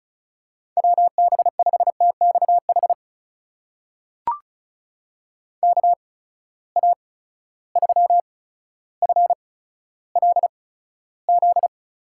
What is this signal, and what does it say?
Morse code, heard as a keyed tone, about 36 wpm, spaced as for 15 wpm, 700 Hz. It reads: WB5TXH KA3FLZ
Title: USA Call Signs - 6 Characters - 8x Character Spacing - 35wpm